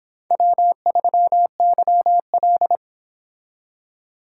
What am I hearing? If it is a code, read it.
Morse code, heard as a keyed tone, about 26 wpm, 700 Hz. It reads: W3YL